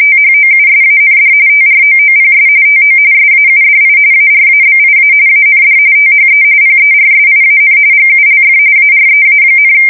RTTY, about 45.45 baud. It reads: THOUSAND TROOPS TO SAUDI ARABIA AND THE PERSIAN GULF AREA,
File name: fsk170 / USS-Missouri